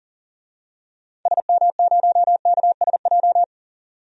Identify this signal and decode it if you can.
Morse code, heard as a keyed tone, about 38 wpm, 700 Hz. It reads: SM0KSJ
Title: DXCC-100 Call Signs - 6 Characters - 40wpm